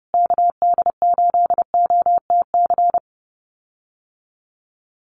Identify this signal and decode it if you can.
Morse code, heard as a keyed tone, about 30 wpm, 700 Hz. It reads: KD8OTC
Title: USA Call Signs Repeated 3x 30wpm